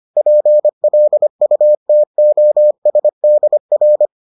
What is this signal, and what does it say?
Morse code, heard as a keyed tone, about 25 wpm, 600 Hz. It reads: PLUTOSDR